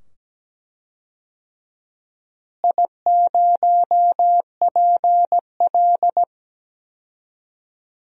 Morse code, heard as a keyed tone, about 17 wpm, 700 Hz. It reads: I0PL
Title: DXCC-100 Call Signs - 4 Characters - Repeated 2x - 17wpm